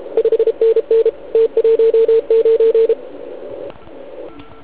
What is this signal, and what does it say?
Morse code, heard as a keyed tone, about 32 wpm, 450 Hz. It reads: 5NN T19
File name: Archiv / CUC149 / OK1IF CUC149